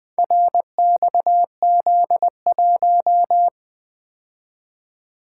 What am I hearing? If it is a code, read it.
Morse code, heard as a keyed tone, about 20 wpm, 700 Hz. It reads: RXZ1
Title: Four Letter-Number Combinations - 20wpm